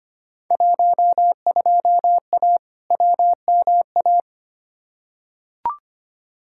Morse code, heard as a keyed tone, about 25 wpm, 700 Hz. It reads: 12A WMA
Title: Field Day Exchanges Encore - 25wpm